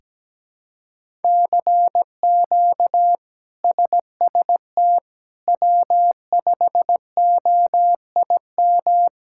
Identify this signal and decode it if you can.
Morse code, heard as a keyed tone, about 17 wpm, 700 Hz. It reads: CQ SST W5OIM
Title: K1USN SST QSO Practice - 17wpm